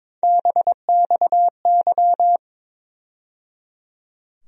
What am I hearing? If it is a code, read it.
Morse code, heard as a keyed tone, about 22 wpm, 700 Hz. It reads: BXY
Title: USA Call Sign Postfixes - 3 Letters 22wpm